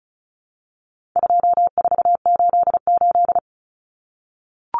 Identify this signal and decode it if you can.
Morse code, heard as a keyed tone, about 34 wpm, 700 Hz. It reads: 2488
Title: Numbers - 4-Digit 35wpm